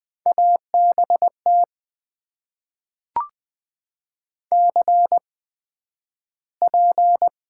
Morse code, heard as a keyed tone, about 20 wpm, spaced as for 11 wpm, 700 Hz. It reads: ABT CP
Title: Top 90 QSO Elements - 8x Character Spacing - 20wpm